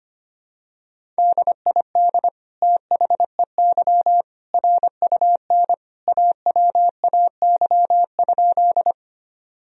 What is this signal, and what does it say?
Morse code, heard as a keyed tone, about 25 wpm, 700 Hz. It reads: DID THEY RUN AWAY?
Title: Sentences from Top 1000 Words - Review Part 1 - 25wpm